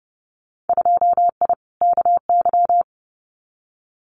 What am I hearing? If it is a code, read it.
Morse code, heard as a keyed tone, about 30 wpm, 700 Hz. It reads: 2I KY
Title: Winter Field Day Exchanges 30wpm